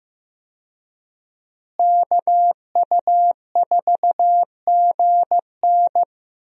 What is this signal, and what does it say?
Morse code, heard as a keyed tone, about 15 wpm, 700 Hz. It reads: KU4GN
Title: USA Call Signs - 5 Characters - Repeated 3x - 15wpm